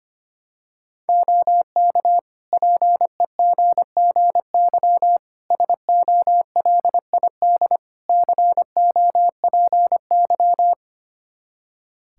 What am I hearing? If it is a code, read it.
Morse code, heard as a keyed tone, about 25 wpm, 700 Hz. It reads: OK PEGGY SOLID COPY
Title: Practice QSO - Exchange 3 - More Info - 25wpm